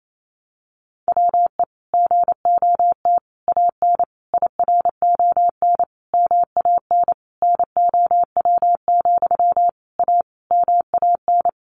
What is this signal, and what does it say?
Morse code, heard as a keyed tone, about 28 wpm, 700 Hz. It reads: WE GOT AN IRON MAN NOW, A MAN
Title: A Sound Of Thunder 28wpm